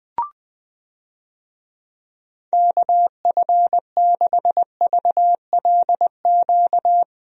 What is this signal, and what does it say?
Morse code, heard as a keyed tone, about 20 wpm, 700 Hz. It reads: KF6VLQ
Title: USA Call Signs - 6 Characters - 20wpm